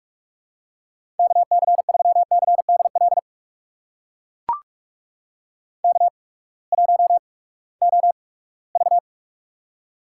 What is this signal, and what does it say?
Morse code, heard as a keyed tone, about 44 wpm, spaced as for 22 wpm, 700 Hz. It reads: KC3CDL K1OV
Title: USA Call Signs - 8x Character Spacing - 45wpm